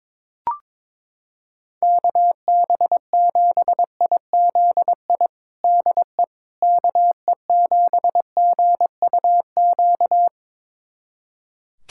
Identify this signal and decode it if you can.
Morse code, heard as a keyed tone, about 22 wpm, 700 Hz. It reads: KB7IZI DE KE7GUQ
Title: USA Callsign DE Callsign - 6 Characters - 22wpm